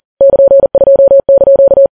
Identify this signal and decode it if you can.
Morse code, heard as a keyed tone, about 40 wpm, 550 Hz. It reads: <KN>2)
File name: Files Web tls / Morse on9fv / morse40kps